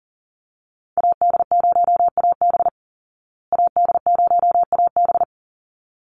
Morse code, heard as a keyed tone, about 37 wpm, 700 Hz. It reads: AD0AB AD0AB
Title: Call Signs Repeated 3x - 40wpm